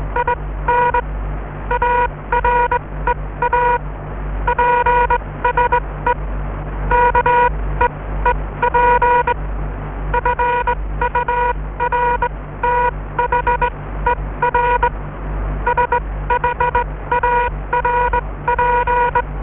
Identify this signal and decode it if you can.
Morse code, heard as a keyed tone, about 20 wpm, spaced as for 16 wpm, 960 Hz. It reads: IN AREA PSE KEEP FURTHER SHARP